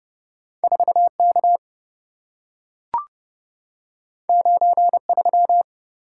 Morse code, heard as a keyed tone, about 27 wpm, 700 Hz. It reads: 4K 93